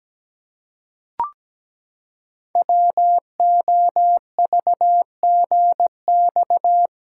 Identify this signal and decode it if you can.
Morse code, heard as a keyed tone, about 17 wpm, 700 Hz. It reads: WOVGX
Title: Code Groups - 5 Letters - 17wpm